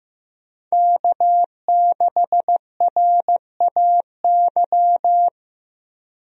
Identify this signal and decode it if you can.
Morse code, heard as a keyed tone, about 15 wpm, 700 Hz. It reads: K6RAY
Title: USA Call Signs - 5 Characters - Repeated 2x - 15wpm